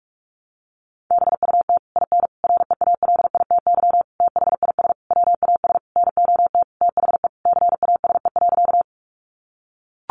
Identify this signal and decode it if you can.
Morse code, heard as a keyed tone, about 44 wpm, 700 Hz. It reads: BUT IN REALITY THIS WAS NOT THE CASE.
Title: Sentences from Top 1200 Words 45wpm